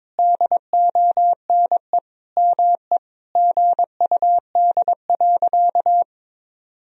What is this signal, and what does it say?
Morse code, heard as a keyed tone, about 22 wpm, 700 Hz. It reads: DONE ME GUD.
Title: Sentences from Top 500 Words and Abbreviations 22wpm